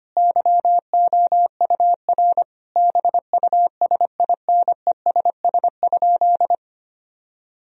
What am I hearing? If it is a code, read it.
Morse code, heard as a keyed tone, about 25 wpm, 700 Hz. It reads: YOUR BUSINESS?
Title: Sentences from Top 1000 Words - Review Part 4 - 25wpm